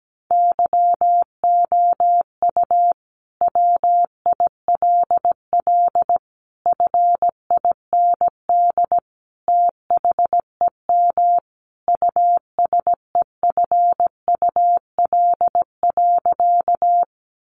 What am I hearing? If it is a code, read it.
Morse code, heard as a keyed tone, about 17 wpm, 700 Hz. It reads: YOU WILL FIND THEM USEFUL.
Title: Sentences from Top 700 Words 17wpm